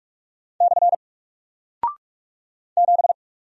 Morse code, heard as a keyed tone, about 44 wpm, 700 Hz. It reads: / 7
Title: Single Character (Letter:Number:Punctuation:Prosign) Alt 45wpm